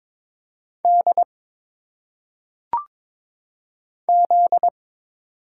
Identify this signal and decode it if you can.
Morse code, heard as a keyed tone, about 22 wpm, 700 Hz. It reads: D Z
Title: Single Letters - Phonetic Pronunciation 22wpm